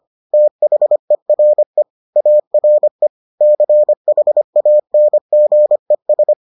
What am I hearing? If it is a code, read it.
Morse code, heard as a keyed tone, about 25 wpm, 600 Hz. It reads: THERE ARE CHANGES